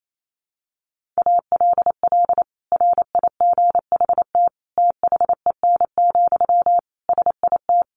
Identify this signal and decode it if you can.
Morse code, heard as a keyed tone, about 28 wpm, 700 Hz. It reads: ALL RIGHT THEN, SIT
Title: Sentences from Top 500 Words 28wpm